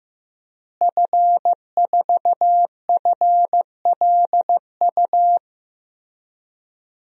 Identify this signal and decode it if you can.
Morse code, heard as a keyed tone, about 16 wpm, 700 Hz. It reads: F4FLU
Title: DXCC-100 Call Signs - 5 Characters - Repeated 3x - 15wpm